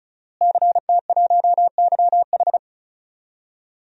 Morse code, heard as a keyed tone, about 36 wpm, 700 Hz. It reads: CT1YH
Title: DXCC-100 Call Signs - 5 Characters - Repeated 2x - 35wpm